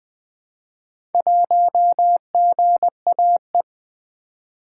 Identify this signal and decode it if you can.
Morse code, heard as a keyed tone, about 20 wpm, 700 Hz. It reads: 1GAE
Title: Four Letter-Number Combinations - 20wpm